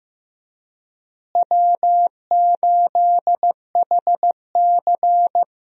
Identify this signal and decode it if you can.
Morse code, heard as a keyed tone, about 15 wpm, 700 Hz. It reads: W8HC
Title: USA Call Signs - 4 Characters - Repeated 2x - 15wpm